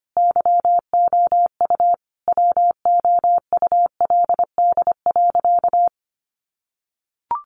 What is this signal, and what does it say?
Morse code, heard as a keyed tone, about 25 wpm, 700 Hz. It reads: YOU WOULD.
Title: Sentences from Top 1500 Words 25wpm